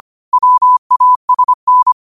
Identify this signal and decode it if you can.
Morse code, heard as a keyed tone, about 25 wpm, 1000 Hz. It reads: WASN
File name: qotdfiles / 25wpm